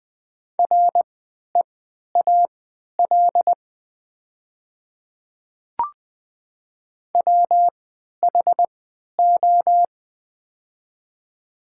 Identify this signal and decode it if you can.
Morse code, heard as a keyed tone, about 20 wpm, spaced as for 11 wpm, 700 Hz. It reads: REAL WHO
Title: 3x Character Spacing - Top 300 Words - 20wpm